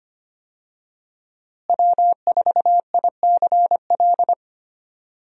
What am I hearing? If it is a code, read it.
Morse code, heard as a keyed tone, about 25 wpm, 700 Hz. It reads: W4ICL
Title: USA Call Signs - 5 Characters - Repeated 3x - 25wpm